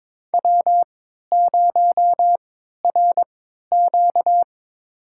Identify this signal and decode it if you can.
Morse code, heard as a keyed tone, about 22 wpm, spaced as for 13 wpm, 700 Hz. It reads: W0RQ
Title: USA Call Signs - 4 Characters - 3x Character Spacing - 22wpm